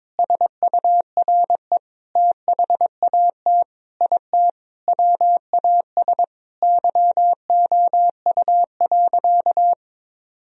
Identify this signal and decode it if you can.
Morse code, heard as a keyed tone, about 22 wpm, 700 Hz. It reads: SURE THAT IT WAS YOU.
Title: Sentences from Top 1000 Words - Review Part 4 - 22wpm